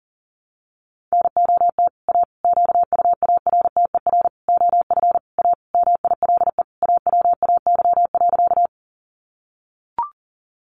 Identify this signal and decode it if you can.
Morse code, heard as a keyed tone, about 40 wpm, 700 Hz. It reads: NOT A QUARTER OF A MILE AWAY.
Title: Sentences from Top 1300 Words 40wpm